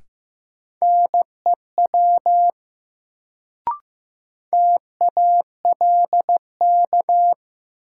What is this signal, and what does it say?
Morse code, heard as a keyed tone, about 15 wpm, 700 Hz. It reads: NEW TALK
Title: Sets of 2 Words - New Word Drill - 15wpm